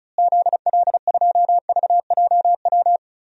Morse code, heard as a keyed tone, about 36 wpm, 700 Hz. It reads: ZL2VJW
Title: DXCC-100 Call Signs - 6 Characters - Repeated 3x - 35wpm